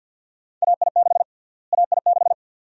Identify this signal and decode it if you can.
Morse code, heard as a keyed tone, about 53 wpm, 700 Hz. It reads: AI6 AI6
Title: USA Call Sign Prefixes - Repeated 2x - 50wpm